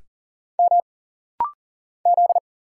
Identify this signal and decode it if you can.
Morse code, heard as a keyed tone, about 40 wpm, 700 Hz. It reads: M Z
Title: Single Letter-Number Alt - Rapid-Fire 40wpm